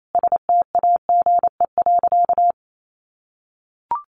Morse code, heard as a keyed tone, about 28 wpm, 700 Hz. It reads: STAGE.
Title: Sentences from Top 1200 Words 28wpm